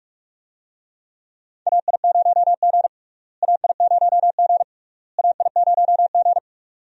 Morse code, heard as a keyed tone, about 44 wpm, 700 Hz. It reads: AI0G AI0G AI0G
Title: USA Call Signs - 4 Characters - Repeated 3x - 45wpm